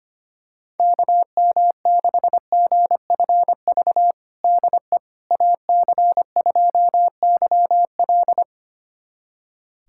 Morse code, heard as a keyed tone, about 25 wpm, 700 Hz. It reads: KM6GFV DE AC2YL
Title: USA Callsign DE Callsign 25wpm